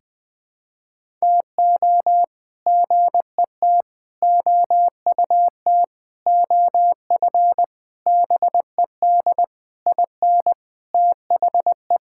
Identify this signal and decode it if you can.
Morse code, heard as a keyed tone, about 20 wpm, 700 Hz. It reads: TO GET OUT OF BED IN THE
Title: Sets of 8 Words - 20wpm